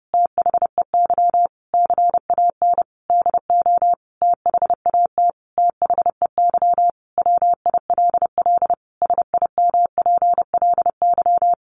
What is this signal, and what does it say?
Morse code, heard as a keyed tone, about 30 wpm, 700 Hz. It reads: THEY CAN DO THAT THEY WILL SIMPLY
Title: Sentences from Top 600 Words 30wpm